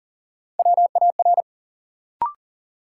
Morse code, heard as a keyed tone, about 40 wpm, 700 Hz. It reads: WAR
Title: Sets of 5 Words - New Word Drill - 40wpm